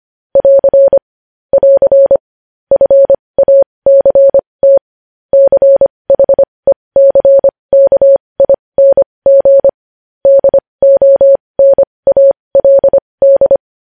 Morse code, heard as a keyed tone, about 25 wpm, spaced as for 20 wpm, 550 Hz. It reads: <AR> <AR> FACT CHECKING DONALD
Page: news-headlines-in-morse-code-at-20-wpm